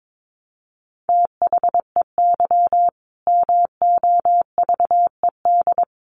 Morse code, heard as a keyed tone, about 22 wpm, 700 Hz. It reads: THEY MOVED
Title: A Sound Of Thunder 22wpm